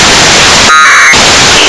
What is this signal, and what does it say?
Packet radio (AX.25, as in APRS).
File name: xw1 fm afsk